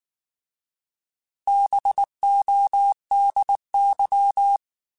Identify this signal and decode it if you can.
Morse code, heard as a keyed tone, about 19 wpm, 800 Hz. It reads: BODY
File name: Alphabetical / Alphabetical mp3s / Bs @18